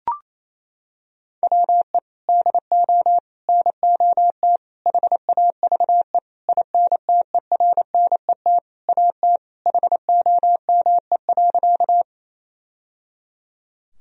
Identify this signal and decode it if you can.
Morse code, heard as a keyed tone, about 28 wpm, 700 Hz. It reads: WE DO NOT HAVE INTERNET AT HOME.
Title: Sentences from Top 1000 Words - Review Part 4 - 28wpm